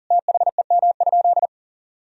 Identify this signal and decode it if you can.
Morse code, heard as a keyed tone, about 40 wpm, 700 Hz. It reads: THEM?
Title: Sentences from Top 1000 Words - Review Part 4 - 40wpm